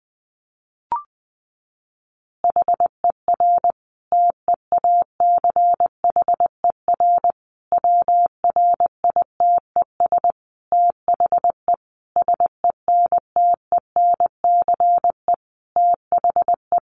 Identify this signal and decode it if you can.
Morse code, heard as a keyed tone, about 20 wpm, 700 Hz. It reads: HER TEACHER WRITES THE SENTENCE THE
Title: Additional Easy Phrases 20wpm